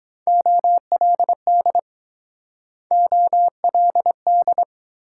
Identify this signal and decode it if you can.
Morse code, heard as a keyed tone, about 23 wpm, 700 Hz. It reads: OLD OLD